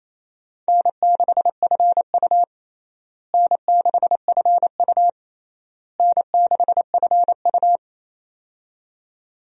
Morse code, heard as a keyed tone, about 28 wpm, 700 Hz. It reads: N6FU N6FU N6FU
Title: USA Call Signs - 4 Characters - Repeated 3x - 28wpm